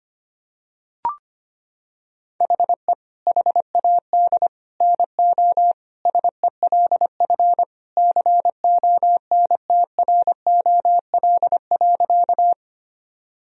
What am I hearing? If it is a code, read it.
Morse code, heard as a keyed tone, about 25 wpm, 700 Hz. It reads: HE HAD NO SELF CONTROL.
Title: Sentences from Top 1000 Words - Review Part 3 - 25wpm